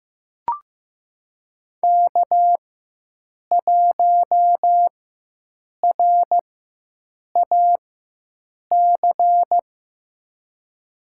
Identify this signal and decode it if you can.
Morse code, heard as a keyed tone, about 15 wpm, spaced as for 7 wpm, 700 Hz. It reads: K1RAC
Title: USA Call Signs - 5 Characters - 4x Character Spacing - 15wpm